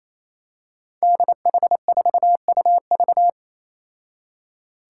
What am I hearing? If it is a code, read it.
Morse code, heard as a keyed tone, about 27 wpm, 700 Hz. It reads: DH4UV
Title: Five Challenging Letter-Number Combinations 28wpm